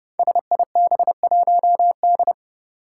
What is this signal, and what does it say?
Morse code, heard as a keyed tone, about 30 wpm, 700 Hz. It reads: SIB1D
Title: Five Challenging Letter-Number Combinations 30wpm